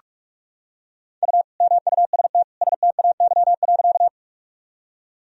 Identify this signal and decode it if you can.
Morse code, heard as a keyed tone, about 46 wpm, 700 Hz. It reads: U MUST STAY.
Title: Sentences from Top 500 Words and Abbreviations 45wpm